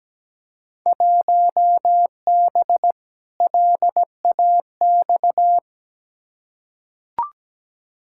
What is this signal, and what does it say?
Morse code, heard as a keyed tone, about 17 wpm, 700 Hz. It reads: 1B LAX
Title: Field Day Exchanges - 17wpm